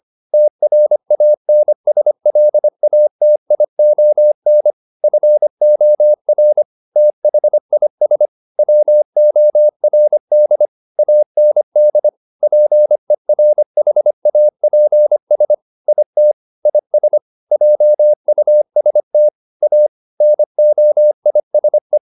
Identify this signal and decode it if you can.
Morse code, heard as a keyed tone, about 25 wpm, 600 Hz. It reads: TRANSLATION FOR THIS WORD AND PERHAPS IT IS JUST A NOISE